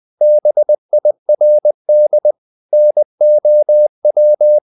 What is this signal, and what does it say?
Morse code, heard as a keyed tone, about 20 wpm, 600 Hz. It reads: BIRD NOW
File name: the old man and the sea / the old man and the sea part 1 20wpm